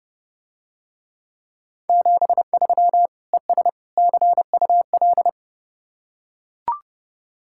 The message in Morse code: 73 ES CUL

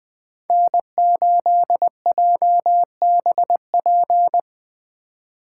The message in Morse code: N8JBP

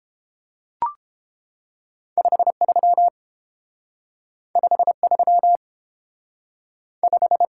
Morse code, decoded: 53 53 5

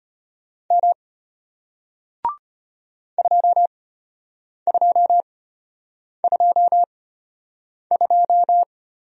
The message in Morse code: M 2222

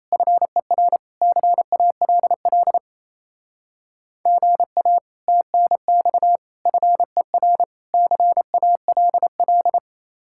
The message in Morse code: FER CALL GA TNX FER CALL

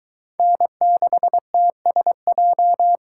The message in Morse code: N6TSJ

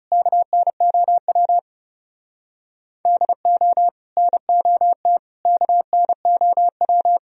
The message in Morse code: KNOW DO NOT KNOW

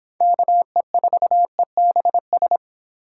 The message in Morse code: KE4EBS

